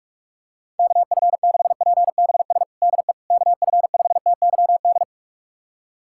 Morse code, heard as a keyed tone, about 46 wpm, 700 Hz. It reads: KF6PBS DE KF5TYD
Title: USA Callsign DE Callsign - 6 Characters - 45wpm